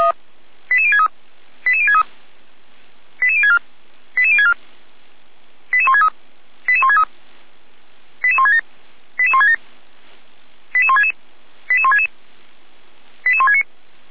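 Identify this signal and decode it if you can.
Five-tone selective call (ZVEI), heard as quick runs of five tones, tones of about 70 ms. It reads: 88063 88063 88064 88064 88163 88163 88167 88167 88166 88166 88169
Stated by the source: ZVEI five-tone call 88402 received by SDR.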